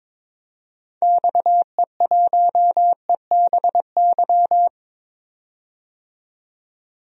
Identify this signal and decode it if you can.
Morse code, heard as a keyed tone, about 22 wpm, 700 Hz. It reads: XE1EBY
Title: DXCC-100 Call Signs - 6 Characters - 22wpm